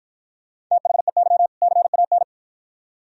Morse code, heard as a keyed tone, about 50 wpm, 700 Hz. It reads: THEY CAN